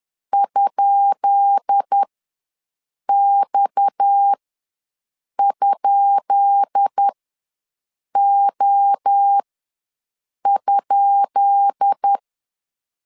Morse code, heard as a keyed tone, about 10 wpm, spaced as for 6 wpm, 800 Hz. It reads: ?X?O?
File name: LEZIONI / 10 SIGNS / 10B INT ws 10wpm 4134